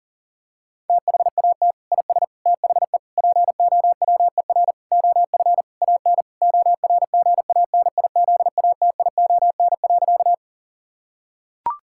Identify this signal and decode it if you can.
Morse code, heard as a keyed tone, about 40 wpm, 700 Hz. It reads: THAT IS THE POWER OF AN ORGANIZATION.
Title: Sentences from Top 1000 Words - Review Part 5 - 40wpm